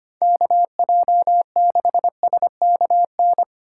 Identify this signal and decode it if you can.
Morse code, heard as a keyed tone, about 32 wpm, 700 Hz. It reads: KJ6SKN